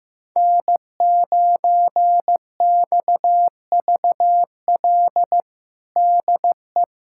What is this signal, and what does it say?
Morse code, heard as a keyed tone, about 15 wpm, 700 Hz. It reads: N9XVL DE
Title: USA Callsign DE Callsign - 5 Characters - 15wpm